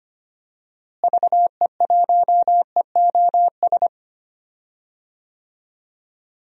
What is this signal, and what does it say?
Morse code, heard as a keyed tone, about 25 wpm, 700 Hz. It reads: VE1EOS
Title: DXCC-100 Call Signs - 6 Characters - Repeated 3x - 25wpm